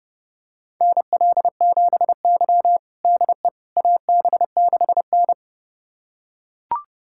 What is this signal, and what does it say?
Morse code, heard as a keyed tone, about 30 wpm, 700 Hz. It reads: NL7Y DE AB6N
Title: USA Callsign DE Callsign - 4 Characters - 30wpm